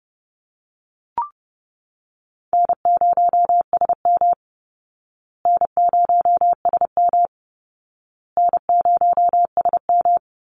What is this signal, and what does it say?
Morse code, heard as a keyed tone, about 30 wpm, 700 Hz. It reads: N0SM N0SM N0SM
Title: Call Signs Repeated 2x - 30wpm